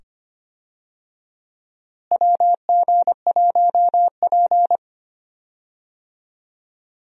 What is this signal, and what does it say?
Morse code, heard as a keyed tone, about 25 wpm, 700 Hz. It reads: WG1P
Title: USA Call Signs - 4 Characters - Repeated 2x - 25wpm